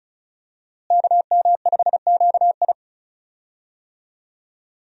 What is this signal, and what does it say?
Morse code, heard as a keyed tone, about 36 wpm, 700 Hz. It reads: KM5QI
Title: USA Call Signs - 5 Characters - Repeated 3x - 35wpm